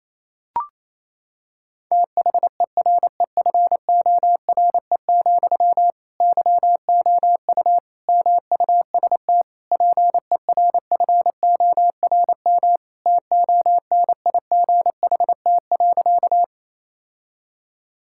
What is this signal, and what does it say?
Morse code, heard as a keyed tone, about 28 wpm, 700 Hz. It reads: THEREFORE, YOU MUST PERFORM TONIGHT.
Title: Sentences from Top 1000 Words - Review Part 5 - 28wpm